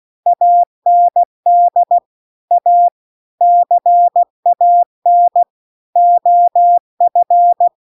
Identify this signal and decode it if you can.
Morse code, heard as a keyed tone, about 16 wpm, 700 Hz. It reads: AND A CAN OF